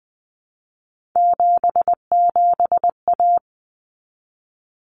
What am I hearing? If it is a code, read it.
Morse code, heard as a keyed tone, about 20 wpm, 700 Hz. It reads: 77A
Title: Common Cut Numbers - 3-Digit 20wpm